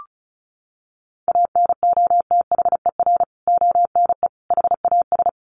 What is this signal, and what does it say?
Morse code, heard as a keyed tone, about 34 wpm, 700 Hz. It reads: ANOTHER ONE HAS
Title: Additional Easy Phrases 35wpm